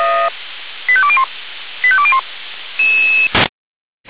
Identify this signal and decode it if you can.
Five-tone selective call (ZVEI), heard as quick runs of five tones, tones of about 70 ms. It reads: 85201 85201
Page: ZVEI